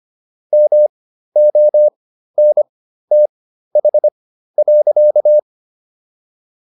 Morse code, heard as a keyed tone, about 25 wpm, spaced as for 13 wpm, 600 Hz. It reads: MONTH.